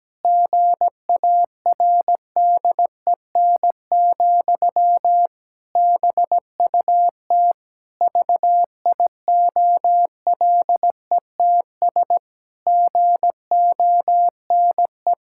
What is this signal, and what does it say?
Morse code, heard as a keyed tone, about 17 wpm, 700 Hz. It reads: GARDEN, BUT VIOLETS GONE
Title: Pangram Challenge v2 - 17wpm